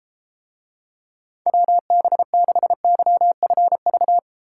Morse code, heard as a keyed tone, about 30 wpm, 700 Hz. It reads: WB6YFV